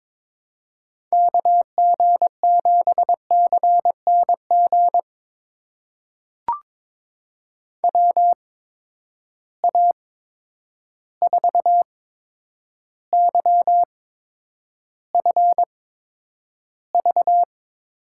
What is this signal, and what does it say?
Morse code, heard as a keyed tone, about 22 wpm, spaced as for 9 wpm, 700 Hz. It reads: KG7CNG WA4YFV
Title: USA Call Signs - 6 Characters - 8x Character Spacing - 22wpm